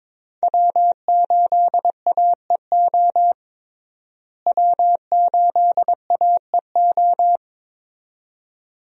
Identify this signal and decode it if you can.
Morse code, heard as a keyed tone, about 22 wpm, 700 Hz. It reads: W8AEO W8AEO
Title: USA Call Signs - 5 Characters - Repeated 2x - 22wpm